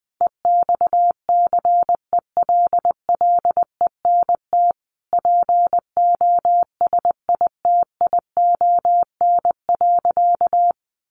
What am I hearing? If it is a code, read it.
Morse code, heard as a keyed tone, about 20 wpm, 700 Hz. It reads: EXCELLENT POSITION.
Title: Sentences from Top 1000 Words 20wpm